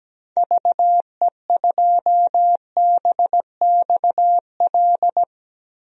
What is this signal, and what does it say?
Morse code, heard as a keyed tone, about 17 wpm, 700 Hz. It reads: VE2BXL
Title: DXCC-100 Call Signs - 6 Characters - Repeated 2x - 17wpm